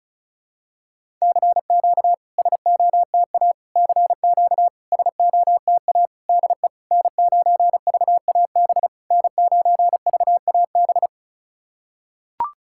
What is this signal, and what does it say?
Morse code, heard as a keyed tone, about 36 wpm, 690 Hz. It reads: CQ SOTA CQ SOTA DE N9VAB N9VAB
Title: SOTA QSO Practice 35wpm